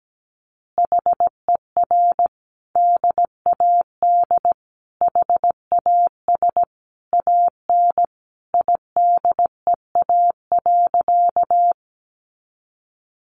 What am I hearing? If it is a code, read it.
Morse code, heard as a keyed tone, about 17 wpm, 700 Hz. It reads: HER DAD HAS AN IDEA.
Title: Easy Phrases 17wpm